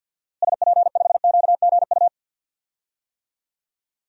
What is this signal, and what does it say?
Morse code, heard as a keyed tone, about 49 wpm, 700 Hz. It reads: SP5QGU
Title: DXCC-100 Call Signs - 6 Characters - Repeated 3x - 50wpm